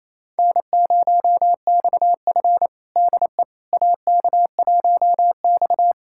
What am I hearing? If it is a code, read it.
Morse code, heard as a keyed tone, about 28 wpm, 700 Hz. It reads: N0XF DE AK1X
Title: USA Callsign DE Callsign - 4 Characters - 28wpm